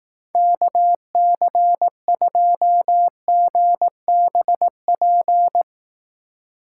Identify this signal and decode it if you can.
Morse code, heard as a keyed tone, about 18 wpm, 700 Hz. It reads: KC2GBP
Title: Speed Racing - USA 2x3 Call Signs 15wpm